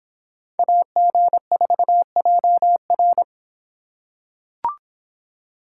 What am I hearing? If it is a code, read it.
Morse code, heard as a keyed tone, about 24 wpm, 700 Hz. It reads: AG4JR